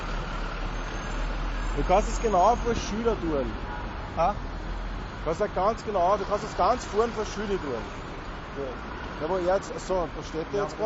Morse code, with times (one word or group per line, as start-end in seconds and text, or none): none